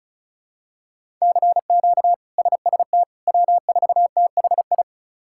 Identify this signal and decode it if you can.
Morse code, heard as a keyed tone, about 36 wpm, 700 Hz. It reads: CQ SST W4THI
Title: K1USN SST QSO Practice - 35wpm